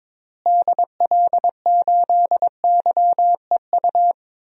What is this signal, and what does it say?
Morse code, heard as a keyed tone, about 22 wpm, 700 Hz. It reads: DL8YEU
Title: DXCC-100 Call Signs - 6 Characters - Repeated 2x - 22wpm